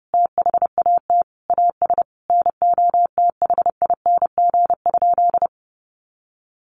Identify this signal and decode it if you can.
Morse code, heard as a keyed tone, about 30 wpm, 700 Hz. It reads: THAT AS NOTHING?
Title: Sentences from Top 1500 Words 30wpm